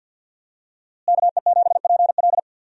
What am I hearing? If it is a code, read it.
Morse code, heard as a keyed tone, about 51 wpm, 700 Hz. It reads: KE7PL
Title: USA Call Signs - 5 Characters - Repeated 2x - 50wpm